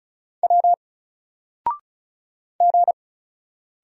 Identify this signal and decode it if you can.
Morse code, heard as a keyed tone, about 35 wpm, 700 Hz. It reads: W G
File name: Single-Letter-Number-35wpm